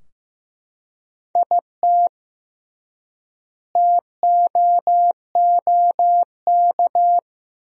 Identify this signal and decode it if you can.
Morse code, heard as a keyed tone, about 15 wpm, 700 Hz. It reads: IT TOOK
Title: 3x Word-Spacing - Sentences from Top 300 Words - 15wpm